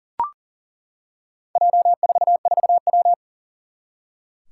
Morse code, heard as a keyed tone, about 40 wpm, 700 Hz. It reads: J44W
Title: Four Challenging Letter-Number Combinations 40wpm